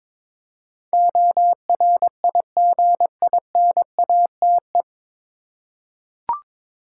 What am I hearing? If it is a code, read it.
Morse code, heard as a keyed tone, about 22 wpm, 700 Hz. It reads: ORIGINATE
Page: Morse Code Ninja practice 22wpm